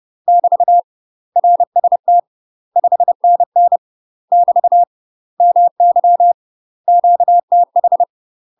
Morse code, heard as a keyed tone, about 30 wpm, 700 Hz. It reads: <BT> RST 5NN <BT> MY QTH
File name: UK MASTER QSOs TRAINING / 30WPM / UK MASTER QSOs 0047